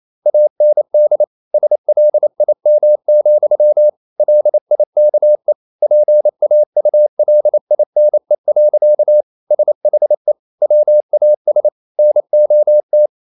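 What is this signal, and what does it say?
Morse code, heard as a keyed tone, about 28 wpm, 600 Hz. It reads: AND SLIM, LIKE PAULINE. SHE WAS NOT
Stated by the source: Morse code practice at 28 wpm, text of Ken Follett's novel Never in English, from CW med Gnister.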